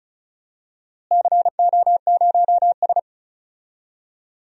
Morse code, heard as a keyed tone, about 34 wpm, 700 Hz. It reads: CO0S
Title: DXCC-100 Call Signs - 4 Characters - Repeated 3x - 35wpm